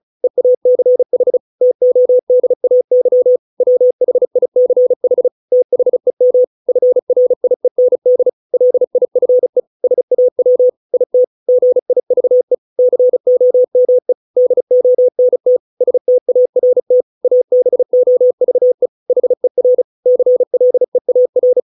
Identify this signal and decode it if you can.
Morse code, heard as a keyed tone, about 34 wpm, 500 Hz. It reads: EACH TODAY WHICH THEM FRIEND LIFE SAW IT GIVE COME DONT START ABOVE HER CLEAR
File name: 632 words / 632 Words 35WPM